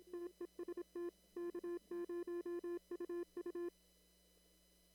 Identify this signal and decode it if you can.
Morse code, heard as a keyed tone, about 27 wpm, 340 Hz. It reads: TEST K0UU
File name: CW / kt5j wpx cw-op-k5pi